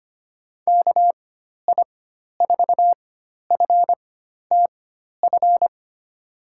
Morse code, heard as a keyed tone, about 25 wpm, spaced as for 12 wpm, 700 Hz. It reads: KI4FTF
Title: USA Call Signs - 6 Characters - 4x Character Spacing - 25wpm